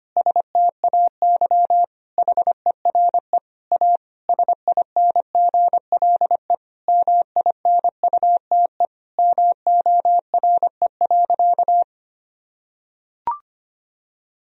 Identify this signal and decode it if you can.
Morse code, heard as a keyed tone, about 25 wpm, 700 Hz. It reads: STAY HERE A SINGLE MINUTE MORE.
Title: Sentences from Top 1000 Words - Review Part 4 - 25wpm